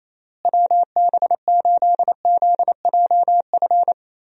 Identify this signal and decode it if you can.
Morse code, heard as a keyed tone, about 28 wpm, 700 Hz. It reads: WB8ZJF
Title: USA Call Signs - 6 Characters - Repeated 3x - 28wpm